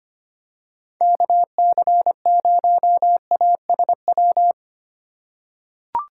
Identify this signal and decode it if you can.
Morse code, heard as a keyed tone, about 25 wpm, 700 Hz. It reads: KC0ASW
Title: USA Call Signs - 2x Character Spacing - 25wpm